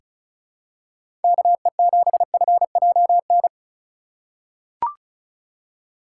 Morse code, heard as a keyed tone, about 36 wpm, 700 Hz. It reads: KE7FJN
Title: USA Call Signs - 6 Characters - 3x Character Spacing - 35wpm